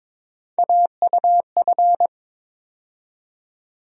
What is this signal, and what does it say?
Morse code, heard as a keyed tone, about 21 wpm, 700 Hz. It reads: AUF
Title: USA Call Sign Postfixes - 3 Letters 22wpm